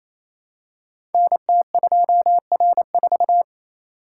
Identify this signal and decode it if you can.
Morse code, heard as a keyed tone, about 27 wpm, 700 Hz. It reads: NT2R4
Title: Code Groups - 5 Letter:Numbers - 28wpm